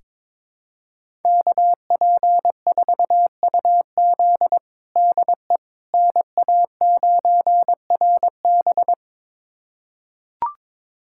Morse code, heard as a keyed tone, about 22 wpm, 700 Hz. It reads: KP4UZ DE NA9RB
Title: USA Callsign DE Callsign - 5 Characters - 22wpm